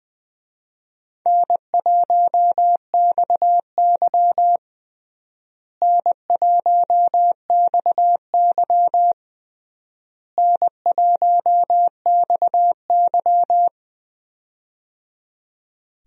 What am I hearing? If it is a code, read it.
Morse code, heard as a keyed tone, about 21 wpm, 700 Hz. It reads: N1XY N1XY N1XY
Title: USA Call Signs - 4 Characters - Repeated 3x - 20wpm